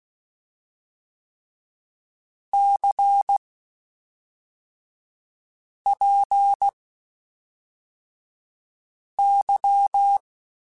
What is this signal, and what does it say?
Morse code, heard as a keyed tone, about 16 wpm, spaced as for 3 wpm, 800 Hz. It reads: CPY
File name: (spaced) / QSOs Spaced-mp3s / WD0FBT 1 @5 sp